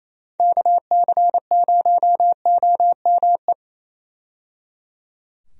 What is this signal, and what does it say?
Morse code, heard as a keyed tone, about 28 wpm, 700 Hz. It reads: KC0OME
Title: USA Call Signs - 6 Characters - 28wpm